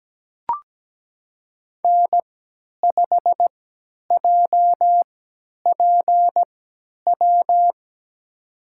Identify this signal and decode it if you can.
Morse code, heard as a keyed tone, about 17 wpm, spaced as for 10 wpm, 700 Hz. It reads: N5JPW
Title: USA Call Signs - 3x Character Spacing - 17wpm